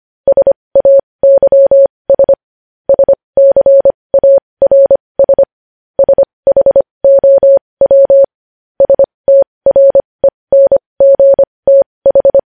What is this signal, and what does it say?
Morse code, heard as a keyed tone, about 25 wpm, spaced as for 20 wpm, 550 Hz. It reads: SAYS SCARS SHOW STRENGTH
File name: podcast / News headlines-1x-25x20